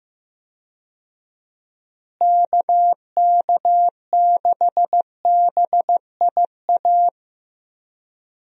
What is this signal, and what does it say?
Morse code, heard as a keyed tone, about 15 wpm, 700 Hz. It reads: KK6BIA
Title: USA Call Signs - 6 Characters - Repeated 3x - 15wpm